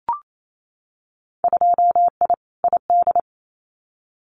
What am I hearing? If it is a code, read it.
Morse code, heard as a keyed tone, about 28 wpm, 700 Hz. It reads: 2I ID
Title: Winter Field Day Exchanges Encore 28wpm